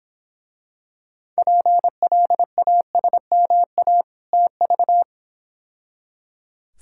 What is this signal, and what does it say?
Morse code, heard as a keyed tone, about 25 wpm, 700 Hz. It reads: PLASMA TV
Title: Difficult Words - Speed-Racing with Hints - 20wpm